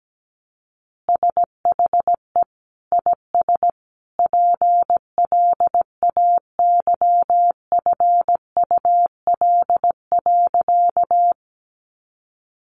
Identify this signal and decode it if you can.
Morse code, heard as a keyed tone, about 17 wpm, 700 Hz. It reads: SHE IS PLAYFUL.
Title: Easy Phrases 17wpm